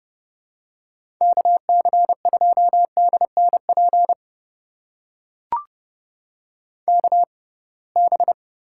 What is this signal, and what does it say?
Morse code, heard as a keyed tone, about 30 wpm, spaced as for 23 wpm, 700 Hz. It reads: KC2DNP KB
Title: USA Call Signs - 6 Characters - 6x Character Spacing - 30wpm